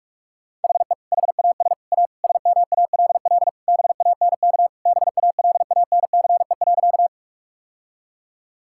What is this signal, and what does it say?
Morse code, heard as a keyed tone, about 46 wpm, 700 Hz. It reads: HE HAS A SMALL BANK BALANCE.
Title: Sentences from Top 1000 Words - Review Part 6 - 45wpm